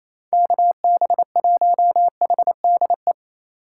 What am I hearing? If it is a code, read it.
Morse code, heard as a keyed tone, about 28 wpm, 700 Hz. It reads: KB1HDE
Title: USA Call Signs - 6 Characters - 3x Character Spacing - 28wpm